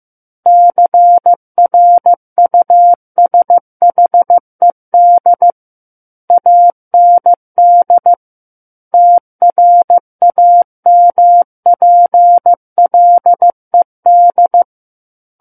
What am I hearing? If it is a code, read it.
Morse code, heard as a keyed tone, about 15 wpm, 700 Hz. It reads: CRUSHED AND TRAMPLED
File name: Chapter 6 - 15 WPM